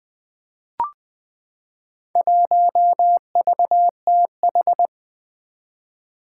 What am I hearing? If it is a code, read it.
Morse code, heard as a keyed tone, about 20 wpm, 700 Hz. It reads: 1VTH
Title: Four Challenging Letter-Number Combinations 20wpm